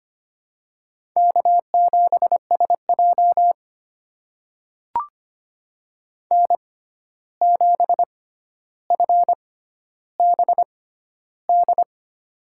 Morse code, heard as a keyed tone, about 25 wpm, spaced as for 12 wpm, 700 Hz. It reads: K7SJ N7FBD
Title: USA Call Signs - 6x Character Spacing - 25wpm